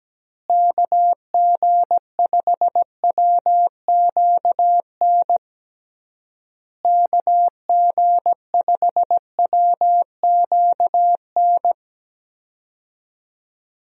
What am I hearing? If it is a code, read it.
Morse code, heard as a keyed tone, about 17 wpm, 700 Hz. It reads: KG5WQN KG5WQN
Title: USA Call Signs - 6 Characters - Repeated 3x - 17wpm